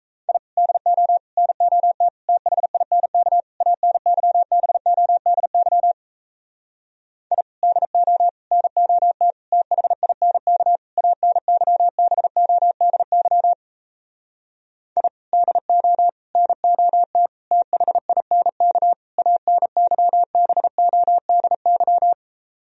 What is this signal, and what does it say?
Morse code, heard as a keyed tone, about 38 wpm, 700 Hz. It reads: I DO NOT THINK ANYBODY I DO NOT THINK ANYBODY I DO NOT THINK ANYBODY